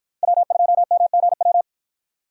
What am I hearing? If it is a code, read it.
Morse code, heard as a keyed tone, about 53 wpm, 700 Hz. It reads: W2MGW